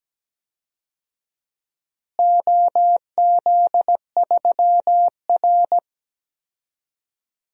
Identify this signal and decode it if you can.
Morse code, heard as a keyed tone, about 17 wpm, 700 Hz. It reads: OZ3R
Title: DXCC-100 Call Signs - 4 Characters - Repeated 2x - 17wpm